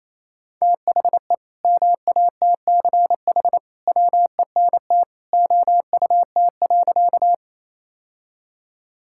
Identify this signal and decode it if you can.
Morse code, heard as a keyed tone, about 28 wpm, 700 Hz. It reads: THE MATCH WENT OUT.
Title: Sentences from Top 1600 Words 28wpm